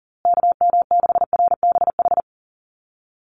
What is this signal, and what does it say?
Morse code, heard as a keyed tone, about 40 wpm, 700 Hz. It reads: KM6RBH
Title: USA Call Signs Repeated 3x 40wpm